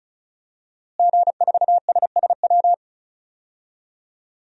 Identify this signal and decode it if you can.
Morse code, heard as a keyed tone, about 36 wpm, 700 Hz. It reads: G4SSW
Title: DXCC-100 Call Signs - 5 Characters - Repeated 3x - 35wpm